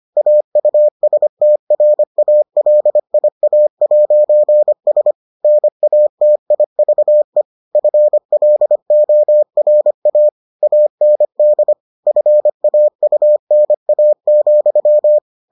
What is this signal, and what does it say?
Morse code, heard as a keyed tone, about 25 wpm, 600 Hz. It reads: AUSTRALIA'S NATIVE FLORA AND FAUNA,